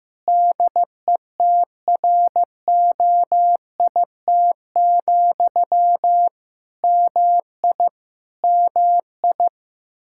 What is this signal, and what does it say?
Morse code, heard as a keyed tone, about 15 wpm, 700 Hz. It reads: DETROIT, MI MI
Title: Practice QSO - Exchange 1 - Beginning - 15wpm